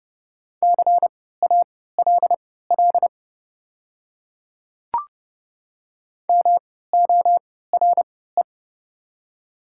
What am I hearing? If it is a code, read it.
Morse code, heard as a keyed tone, about 30 wpm, spaced as for 17 wpm, 700 Hz. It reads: CALL MORE
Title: 3x Character Spacing - Top 200 Words - 30wpm